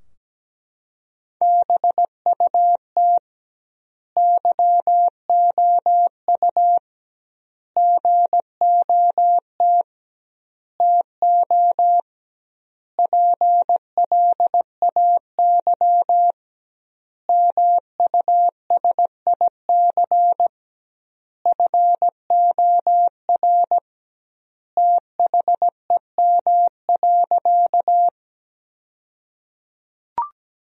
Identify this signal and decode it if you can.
Morse code, heard as a keyed tone, about 17 wpm, 700 Hz. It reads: BUT YOU GOT TO PLAY MUSIC FOR THEM.
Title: 2x Word-Spacing - Sentences from Top 300 Words - 17wpm